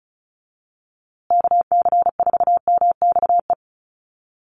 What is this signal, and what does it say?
Morse code, heard as a keyed tone, about 34 wpm, 700 Hz. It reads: KC4MXE
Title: USA Call Signs Repeated 3x 35wpm